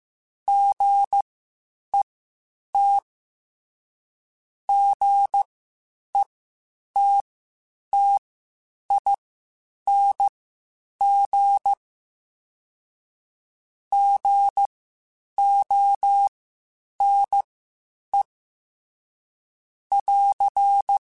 Morse code, heard as a keyed tone, about 15 wpm, spaced as for 8 wpm, 800 Hz. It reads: GET GETTING GONE <AR>
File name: Misc / Misc-mp3s / Gotten @13